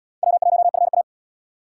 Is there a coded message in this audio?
Yes: Morse code from 0.2 to 1.0 s, heard as a keyed tone, about 71 wpm, 700 Hz. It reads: W1FA